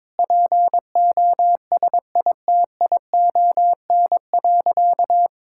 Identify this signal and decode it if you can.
Morse code, heard as a keyed tone, about 22 wpm, 700 Hz. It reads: POSITION.
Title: Sentences from Top 1000 Words - Review Part 5 - 22wpm